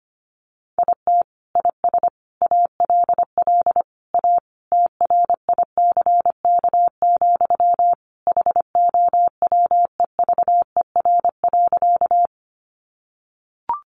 Morse code, heard as a keyed tone, about 25 wpm, 700 Hz. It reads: IT IS ALL A TRICK, HOWEVER.
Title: Sentences from Top 1500 Words 25wpm